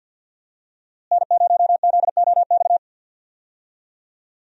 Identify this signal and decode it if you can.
Morse code, heard as a keyed tone, about 52 wpm, 700 Hz. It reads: N0ZOX
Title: USA Call Signs - 5 Characters - Repeated 2x - 50wpm